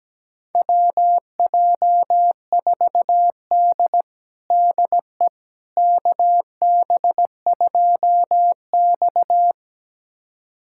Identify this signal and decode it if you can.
Morse code, heard as a keyed tone, about 17 wpm, 700 Hz. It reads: WJ4D DE KB2X
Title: USA Callsign DE Callsign - 4 Characters - 17wpm